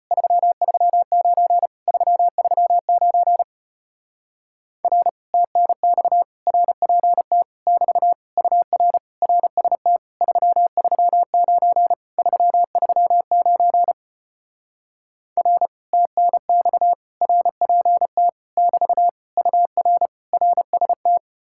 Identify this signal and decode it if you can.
Morse code, heard as a keyed tone, about 33 wpm, 700 Hz. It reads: 339 339 R TNX RPT <BT> UR RST 339 339 R TNX RPT <BT> UR RST